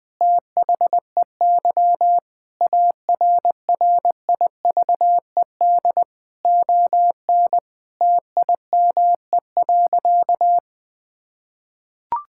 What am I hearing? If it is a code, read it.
Morse code, heard as a keyed tone, about 20 wpm, 700 Hz. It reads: THEY ARRIVED ON TIME.
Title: Sentences with 2-Word Phrases - Spoken Beginning - 20wpm